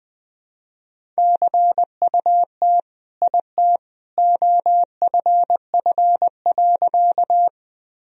Morse code, heard as a keyed tone, about 20 wpm, 700 Hz. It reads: CUT IT OFF.
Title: Sentences from Top 1000 Words - Review Part 1 - 20wpm